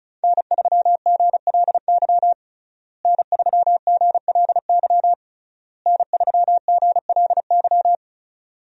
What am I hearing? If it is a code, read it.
Morse code, heard as a keyed tone, about 36 wpm, 700 Hz. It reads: N3GLY N3GLY N3GLY
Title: USA Call Signs - 5 Characters - Repeated 3x - 35wpm